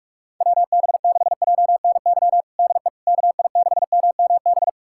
Morse code, heard as a keyed tone, about 46 wpm, 700 Hz. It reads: WB6JNY DE KI6MMB
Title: USA Callsign DE Callsign 45wpm